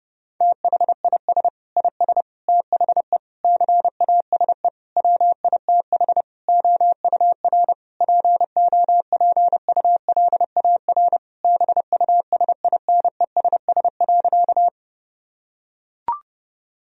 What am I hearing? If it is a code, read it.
Morse code, heard as a keyed tone, about 32 wpm, 700 Hz. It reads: THIS IS THE CASE WITH OUR POPULAR BUSINESS.
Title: Sentences from Top 1000 Words - Review Part 3 - 30wpm